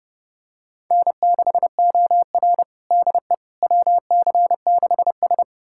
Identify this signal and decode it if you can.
Morse code, heard as a keyed tone, about 32 wpm, 700 Hz. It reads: N6OR DE WC6S
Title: USA Callsign DE Callsign - 4 Characters - 30wpm